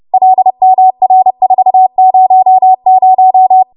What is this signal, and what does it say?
Morse code, heard as a keyed tone, about 30 wpm, 750 Hz. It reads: LMR400
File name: Morse / 30WPM